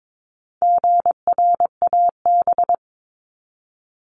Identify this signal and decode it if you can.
Morse code, heard as a keyed tone, about 22 wpm, 700 Hz. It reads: GRAB
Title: Top 1751-2000 Words 22wpm